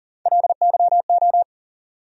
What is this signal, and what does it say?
Morse code, heard as a keyed tone, about 40 wpm, 700 Hz. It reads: LYO
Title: USA Call Sign Postfixes - 3 Letters 40wpm